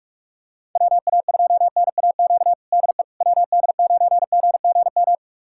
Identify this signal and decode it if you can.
Morse code, heard as a keyed tone, about 46 wpm, 700 Hz. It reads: WA2NAQ DE WD9GGM
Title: USA Callsign DE Callsign - 6 Characters - 45wpm